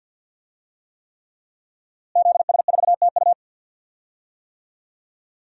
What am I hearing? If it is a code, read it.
Morse code, heard as a keyed tone, about 48 wpm, 700 Hz. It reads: ZS4TU
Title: DXCC-100 Call Signs - 5 Characters - Repeated 3x - 50wpm